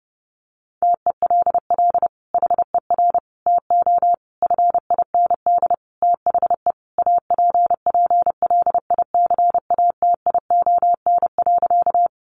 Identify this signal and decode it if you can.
Morse code, heard as a keyed tone, about 30 wpm, 700 Hz. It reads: TELL HER TO FIND THE APPLICATION.
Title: Sentences from Top 1200 Words 30wpm